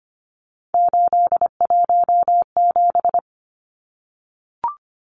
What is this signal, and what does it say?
Morse code, heard as a keyed tone, about 25 wpm, 700 Hz. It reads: 817
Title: Numbers - 3-Digit 25wpm